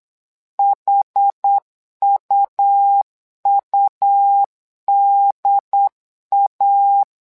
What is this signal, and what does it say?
Morse code, heard as a keyed tone, about 8 wpm, 800 Hz. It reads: HUUDA
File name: LEZIONI / 2 UHDN / 2 UHD 8wpm 2258